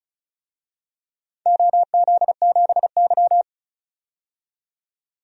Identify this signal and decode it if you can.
Morse code, heard as a keyed tone, about 36 wpm, 700 Hz. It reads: OZ7Y
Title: DXCC-100 Call Signs - 4 Characters - 35wpm